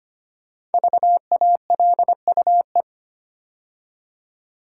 Morse code, heard as a keyed tone, about 25 wpm, 700 Hz. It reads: VALUE
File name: 5 Letter Words - First 3 Letters Given - 25wpm